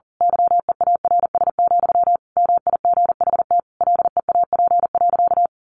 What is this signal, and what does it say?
Morse code, heard as a keyed tone, about 40 wpm, 700 Hz. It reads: YEARS, MIGHT LEAP.
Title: A Sound Of Thunder 40wpm